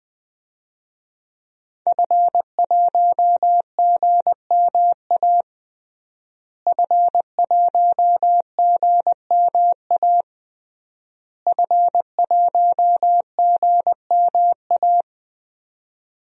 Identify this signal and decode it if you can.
Morse code, heard as a keyed tone, about 20 wpm, 700 Hz. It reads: F1GMA F1GMA F1GMA
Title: DXCC-100 Call Signs - 5 Characters - Repeated 3x - 20wpm